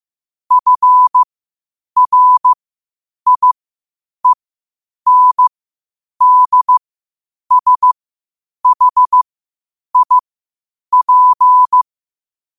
Morse code, heard as a keyed tone, about 15 wpm, spaced as for 8 wpm, 1000 Hz. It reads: FRIENDSHIP
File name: qotdfiles / 8wpm